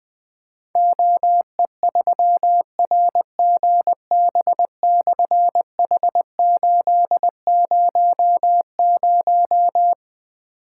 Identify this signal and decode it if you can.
Morse code, heard as a keyed tone, about 21 wpm, 700 Hz. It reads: OE3RGB/H800
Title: Difficult Call Signs 20wpm